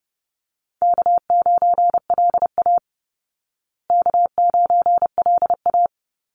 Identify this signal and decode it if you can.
Morse code, heard as a keyed tone, about 30 wpm, 700 Hz. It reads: K9LA K9LA
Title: Easy Call Signs Repeated 3x - 30wpm